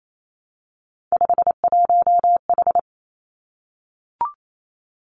Morse code, heard as a keyed tone, about 28 wpm, 700 Hz. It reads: 51H
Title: Three Challenging Letter-Number Combinations 28wpm